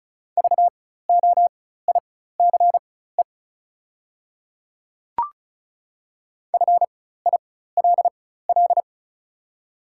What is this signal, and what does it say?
Morse code, heard as a keyed tone, about 35 wpm, spaced as for 16 wpm, 690 Hz. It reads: VOICE FILL
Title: 4x Character Spacing - Top 500 Words - 35wpm